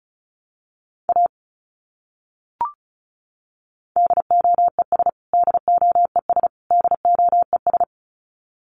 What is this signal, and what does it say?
Morse code, heard as a keyed tone, about 34 wpm, 700 Hz. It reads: A DOES DOES DOES
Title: Top 200 Words Repeated 35wpm